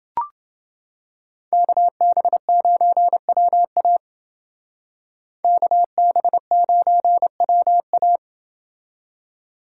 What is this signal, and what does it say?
Morse code, heard as a keyed tone, about 28 wpm, 700 Hz. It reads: KB9WA KB9WA